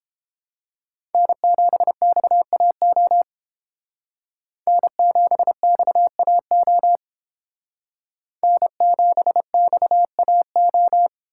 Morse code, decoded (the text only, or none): N7XAO N7XAO N7XAO